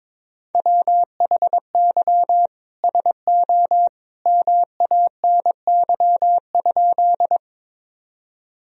Morse code, heard as a keyed tone, about 22 wpm, 700 Hz. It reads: WHY SO MANY?
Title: Sentences from Top 1000 Words - Review Part 1 - 22wpm